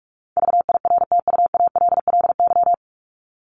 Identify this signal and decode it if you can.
Morse code, heard as a keyed tone, about 44 wpm, 700 Hz. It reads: VIRTUALLY
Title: LY Suffix - Top 100 Words with Base Words 45wpm